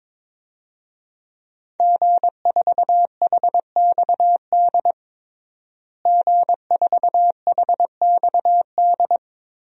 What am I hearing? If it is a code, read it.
Morse code, heard as a keyed tone, about 22 wpm, 700 Hz. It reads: G4HXD G4HXD
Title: DXCC-100 Call Signs - 5 Characters - Repeated 2x - 22wpm